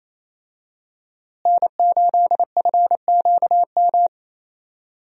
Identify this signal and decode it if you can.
Morse code, heard as a keyed tone, about 28 wpm, 700 Hz. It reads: N8FQM
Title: USA Call Signs - 5 Characters - Repeated 2x - 28wpm